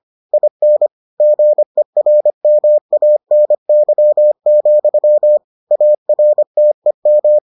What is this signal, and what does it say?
Morse code, heard as a keyed tone, about 25 wpm, 600 Hz. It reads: IN GERMANY, ARTEM